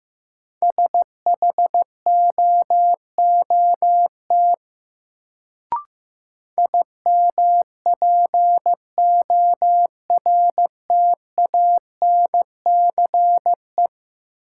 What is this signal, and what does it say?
Morse code, heard as a keyed tone, about 15 wpm, 700 Hz. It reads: SHOOT IMPORTANCE
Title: Sentences from Top 1200 Words - New Word Drill - 15wpm